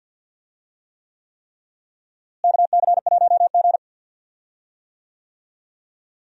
Morse code, heard as a keyed tone, about 46 wpm, 700 Hz. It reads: KC1G
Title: USA Call Signs - 4 Characters - Repeated 3x - 50wpm